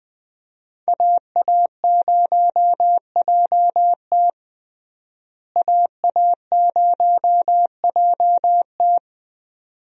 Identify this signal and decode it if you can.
Morse code, heard as a keyed tone, about 21 wpm, 700 Hz. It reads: AA0JT AA0JT
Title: USA Call Signs - 5 Characters - Repeated 2x - 20wpm